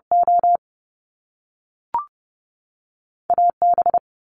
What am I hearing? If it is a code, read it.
Morse code, heard as a keyed tone, about 30 wpm, 700 Hz. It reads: O AB
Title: Top Two Letter Combinations in Words 30wpm